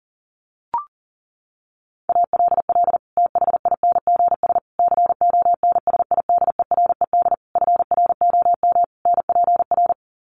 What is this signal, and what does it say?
Morse code, heard as a keyed tone, about 40 wpm, 700 Hz. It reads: ALL THINGS CONSIDERED FROM NPR
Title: Sets of 5 Words Encore 40wpm